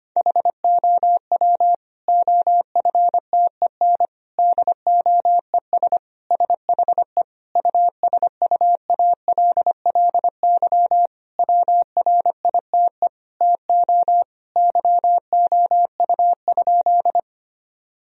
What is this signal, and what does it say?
Morse code, heard as a keyed tone, about 25 wpm, 700 Hz. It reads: HOW OFTEN DOES SHE USUALLY WRITE TO YOU?
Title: Sentences from Top 1000 Words - Review Part 3 - 25wpm